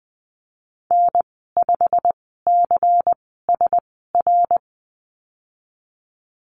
Morse code, heard as a keyed tone, about 19 wpm, spaced as for 14 wpm, 700 Hz. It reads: N5CSR
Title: 128 Lesson 21g - Number 6 - Call Signs 20wpm